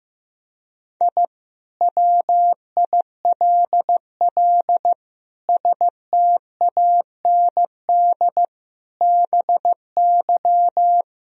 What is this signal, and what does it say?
Morse code, heard as a keyed tone, about 15 wpm, 700 Hz. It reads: I WILL STAND BY
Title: Sentences from Top 1000 Words - Review Part 1 - 15wpm